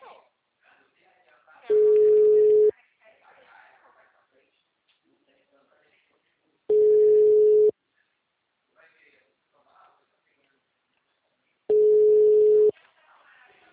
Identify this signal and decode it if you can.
Ringback tone (Europe, 425 Hz), with speech.